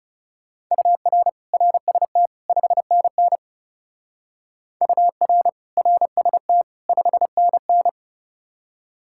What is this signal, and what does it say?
Morse code, heard as a keyed tone, about 35 wpm, 700 Hz. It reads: UR RST 5NN UR RST 5NN
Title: Speed-Racing Common QSO Phrases - 30wpm